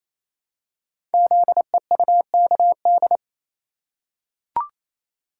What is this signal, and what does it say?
Morse code, heard as a keyed tone, about 27 wpm, 700 Hz. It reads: ZEUKD
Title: Code Groups - 5 Letters - 28wpm